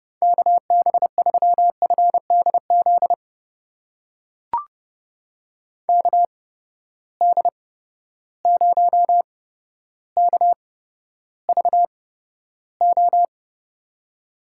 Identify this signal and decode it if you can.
Morse code, heard as a keyed tone, about 31 wpm, spaced as for 13 wpm, 700 Hz. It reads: KB3FDZ KD0KVO
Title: USA Call Signs - 6 Characters - 8x Character Spacing - 30wpm